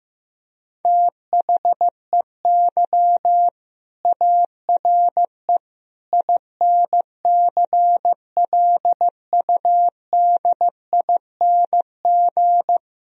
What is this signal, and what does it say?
Morse code, heard as a keyed tone, about 15 wpm, 700 Hz. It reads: THEY ARE INCLUDING
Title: Sentences from Top 1000 Words - Review Part 4 - 15wpm